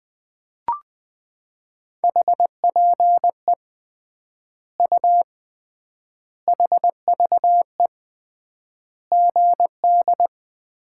Morse code, heard as a keyed tone, about 21 wpm, 700 Hz. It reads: HPE U HVE GD
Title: Common QSO Phrases - 3x Word-Spacing - 20wpm